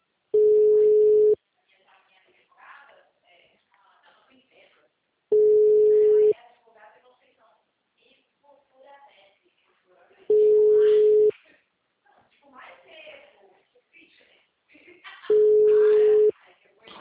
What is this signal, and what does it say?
Ringback tone (Europe, 425 Hz), with speech.